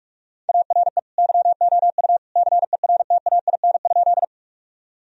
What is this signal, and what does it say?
Morse code, heard as a keyed tone, about 46 wpm, 700 Hz. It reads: ARE YOU CERTAIN?
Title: Sentences from Top 1000 Words - Review Part 2 - 45wpm